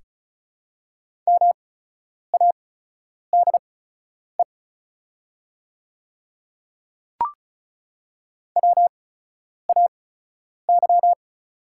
Morse code, heard as a keyed tone, about 34 wpm, spaced as for 10 wpm, 690 Hz. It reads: MADE WAY
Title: 8x Character Spacing - Top 200 Words - 35wpm